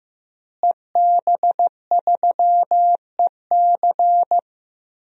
Morse code, heard as a keyed tone, about 15 wpm, 700 Hz. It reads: EB3EC
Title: DXCC-100 Call Signs - 5 Characters - Repeated 2x - 15wpm